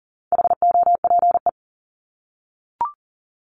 Morse code, heard as a keyed tone, about 40 wpm, 700 Hz. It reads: HOPE
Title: Top 751-1000 Words 40wpm